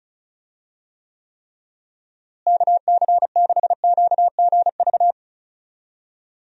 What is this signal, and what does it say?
Morse code, heard as a keyed tone, about 34 wpm, 700 Hz. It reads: KC6QGV
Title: USA Call Signs - 6 Characters - Repeated 3x - 35wpm